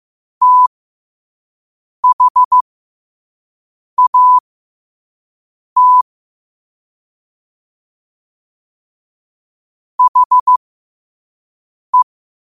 Morse code, heard as a keyed tone, about 15 wpm, spaced as for 5 wpm, 1000 Hz. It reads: THAT HE
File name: qotdfiles / 5wpm